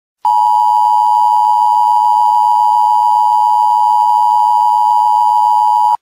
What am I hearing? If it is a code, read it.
Emergency Alert System (SAME) tones (the 853 and 960 Hz attention tone). It is heard from 0.2 to 6.0 s.